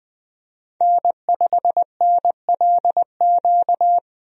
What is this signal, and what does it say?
Morse code, heard as a keyed tone, about 21 wpm, 700 Hz. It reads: N5NLQ
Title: USA Call Signs - 5 Characters - 20wpm